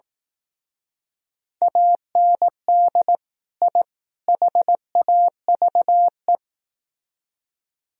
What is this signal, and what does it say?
Morse code, heard as a keyed tone, about 18 wpm, 700 Hz. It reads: AND I HAVE
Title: Speed-Racing - Sets of 3 Words 15wpm